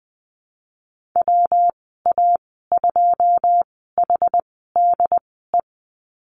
Morse code, heard as a keyed tone, about 20 wpm, spaced as for 15 wpm, 700 Hz. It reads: WA2HDE
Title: 161 Lesson 26g - Letter g - Call Signs 20wpm